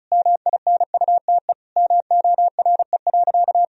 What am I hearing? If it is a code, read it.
Morse code, heard as a keyed tone, about 34 wpm, 700 Hz. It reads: MINUTE MORE.
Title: Sentences from Top 1000 Words - Review Part 4 - 35wpm